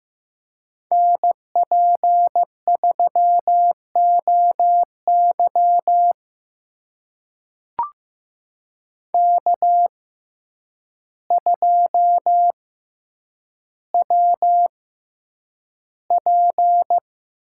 Morse code, heard as a keyed tone, about 15 wpm, spaced as for 8 wpm, 700 Hz. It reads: NP3OY K2WP
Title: USA Call Signs - 6x Character Spacing - 15wpm